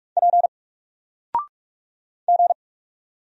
Morse code, heard as a keyed tone, about 48 wpm, 700 Hz. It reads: P G
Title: Single Letter-Number Alt 45wpm